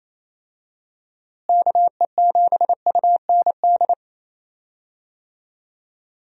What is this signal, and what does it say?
Morse code, heard as a keyed tone, about 28 wpm, 700 Hz. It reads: KE7UND
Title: USA Call Signs - 6 Characters - Repeated 3x - 28wpm